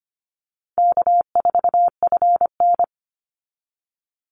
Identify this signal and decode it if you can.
Morse code, heard as a keyed tone, about 25 wpm, 700 Hz. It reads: K4FN
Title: Easy Call Signs Repeated 2x - 25wpm